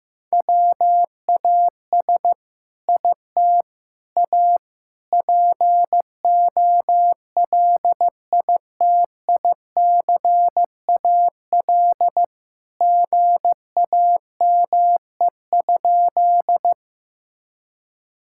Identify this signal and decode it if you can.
Morse code, heard as a keyed tone, about 15 wpm, 700 Hz. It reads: WAS IT A POLITICAL GAME?
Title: Sentences from Top 1000 Words - Review Part 5 - 15wpm